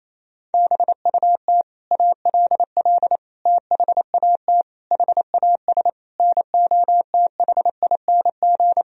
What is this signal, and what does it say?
Morse code, heard as a keyed tone, about 27 wpm, 700 Hz. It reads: BUT ALL THAT HAS NOTHING
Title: Sentences from Top 1000 Words - Review Part 3 - 28wpm